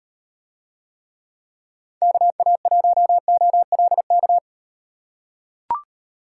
Morse code, heard as a keyed tone, about 38 wpm, 700 Hz. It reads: KA1OLK